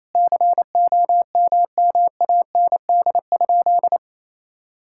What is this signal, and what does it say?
Morse code, heard as a keyed tone, about 27 wpm, 700 Hz. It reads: COMMAND?
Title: Sentences from Top 1600 Words 28wpm